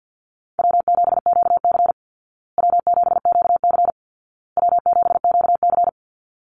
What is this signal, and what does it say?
Morse code, heard as a keyed tone, about 50 wpm, 700 Hz. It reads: W7QC W7QC W7QC
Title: Call Signs Repeated 2x - 50wpm